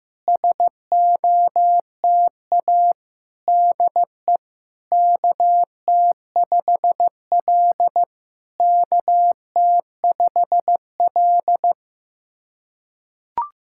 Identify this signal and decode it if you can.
Morse code, heard as a keyed tone, about 16 wpm, 700 Hz. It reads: SOTA DE KT5L KT5L
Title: SOTA QSO Practice 15wpm